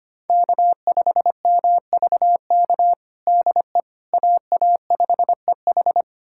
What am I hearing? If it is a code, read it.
Morse code, heard as a keyed tone, about 25 wpm, 700 Hz. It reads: K5MVK DE AA5EH
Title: USA Callsign DE Callsign - 5 Characters - 25wpm